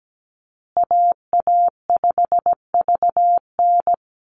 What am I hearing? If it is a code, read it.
Morse code, heard as a keyed tone, about 17 wpm, 700 Hz. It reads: AA5VN
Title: USA Call Signs Repeated 3x 17wpm